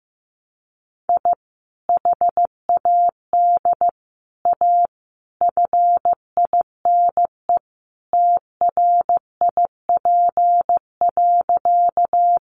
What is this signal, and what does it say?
Morse code, heard as a keyed tone, about 14 wpm, 700 Hz. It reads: I HAD A FINE TRIP.
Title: Sentences from Top 1300 Words 15wpm